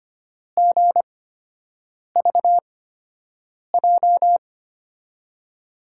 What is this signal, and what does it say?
Morse code, heard as a keyed tone, about 25 wpm, spaced as for 7 wpm, 700 Hz. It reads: GVJ
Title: USA Call Signs - 6 Characters - 8x Character Spacing - 25wpm